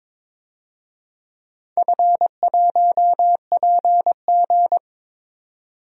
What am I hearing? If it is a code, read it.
Morse code, heard as a keyed tone, about 22 wpm, 700 Hz. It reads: F1PG
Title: DXCC-100 Call Signs - 4 Characters - 22wpm